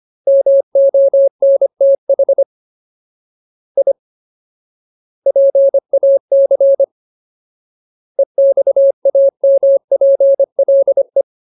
Morse code, heard as a keyed tone, about 25 wpm, 550 Hz. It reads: MONTH I PAC EXAMPLE
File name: Ultimate Word List / 25WPM Ultimate